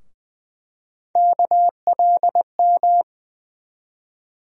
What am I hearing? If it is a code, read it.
Morse code, heard as a keyed tone, about 20 wpm, 700 Hz. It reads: KLM
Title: USA Call Sign Postfixes - 3 Letters 20wpm